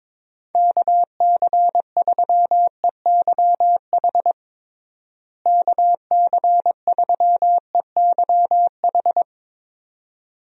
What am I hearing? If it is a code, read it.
Morse code, heard as a keyed tone, about 22 wpm, 700 Hz. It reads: KC3EYH KC3EYH
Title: USA Call Signs - 6 Characters - Repeated 2x - 22wpm